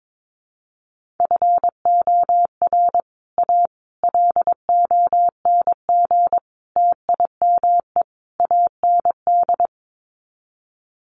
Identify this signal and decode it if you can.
Morse code, heard as a keyed tone, about 22 wpm, 700 Hz. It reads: FOR A LONG TIME AND
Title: Sets of 5 Words Encore2 22wpm